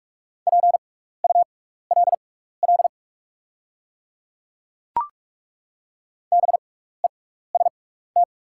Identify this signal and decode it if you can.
Morse code, heard as a keyed tone, about 48 wpm, spaced as for 16 wpm, 690 Hz. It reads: PULL BEST
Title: 6x Character Spacing - Top 500 Words - 45wpm